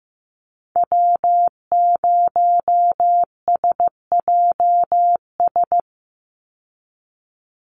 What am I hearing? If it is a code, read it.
Morse code, heard as a keyed tone, about 15 wpm, 700 Hz. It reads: W0SJS
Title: Call Signs Repeated 2x - 15wpm